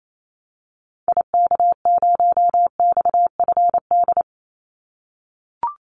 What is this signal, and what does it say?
Morse code, heard as a keyed tone, about 28 wpm, 700 Hz. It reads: IK0XFD
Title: Call Signs - 28wpm